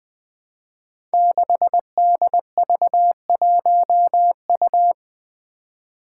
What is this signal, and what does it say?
Morse code, heard as a keyed tone, about 20 wpm, 700 Hz. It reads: 6DV1U
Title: Five Challenging Letter-Number Combinations 20wpm